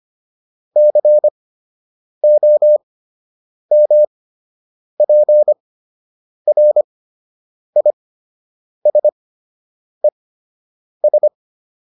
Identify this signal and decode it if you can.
Morse code, heard as a keyed tone, about 25 wpm, spaced as for 8 wpm, 600 Hz. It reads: COMPRISES